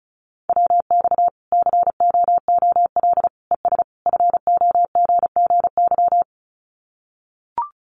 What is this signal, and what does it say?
Morse code, heard as a keyed tone, about 34 wpm, 700 Hz. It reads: WX COOL ES FOGGY
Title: Common QSO Phrases 35wpm